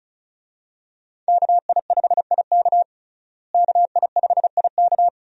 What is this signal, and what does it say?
Morse code, heard as a keyed tone, about 34 wpm, 700 Hz. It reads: KI5IK KI5IK
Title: USA Call Signs - 5 Characters - Repeated 2x - 35wpm